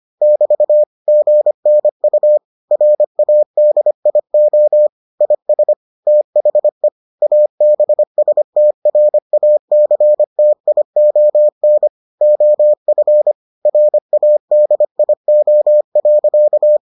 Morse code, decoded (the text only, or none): <BT> GNU RADIO IS THE ABSTRACTION OF RADIO.